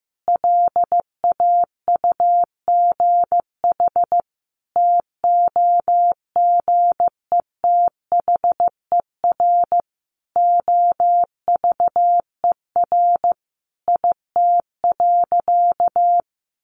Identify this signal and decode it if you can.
Morse code, heard as a keyed tone, about 15 wpm, 700 Hz. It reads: LAUGH TOGETHER OVER IT.
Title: Sentences from Top 1500 Words 15wpm